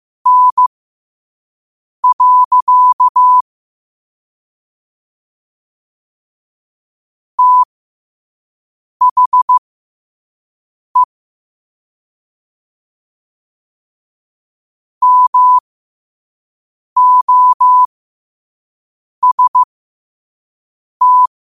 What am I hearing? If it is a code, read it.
Morse code, heard as a keyed tone, about 15 wpm, spaced as for 5 wpm, 1000 Hz. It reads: N. THE MOST
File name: qotdfiles / 5wpm